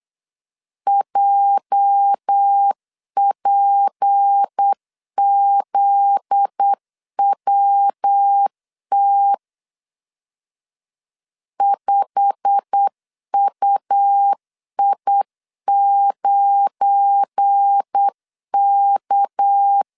Morse code, heard as a keyed tone, about 8 wpm, 800 Hz. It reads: JPZWT 5UI9K
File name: CW / DET / 1 9 1500C S A 8wpm 5637